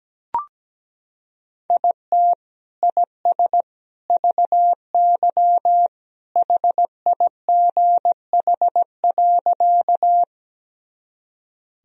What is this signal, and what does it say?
Morse code, heard as a keyed tone, about 17 wpm, 700 Hz. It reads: IT IS VY HIGH.
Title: Sentences from Top 200 Words and Abbreviations 17wpm